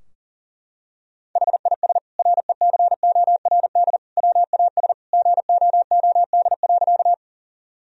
Morse code, heard as a keyed tone, about 40 wpm, 700 Hz. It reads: HIS RECORD WAS GOOD.
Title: Sentences from Top 1000 Words - Review Part 3 - 40wpm